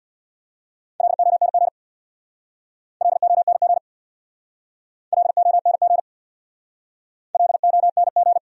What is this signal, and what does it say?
Morse code, heard as a keyed tone, about 63 wpm, 700 Hz. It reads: LONG LONG LONG LONG